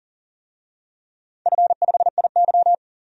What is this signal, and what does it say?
Morse code, heard as a keyed tone, about 42 wpm, 700 Hz. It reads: F5IY
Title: DXCC-100 Call Signs - 4 Characters - 40wpm